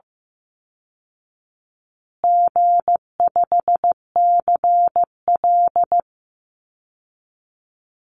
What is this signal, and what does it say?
Morse code, heard as a keyed tone, about 15 wpm, 700 Hz. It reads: G5CL
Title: Easy Call Signs Repeated 3x - 15wpm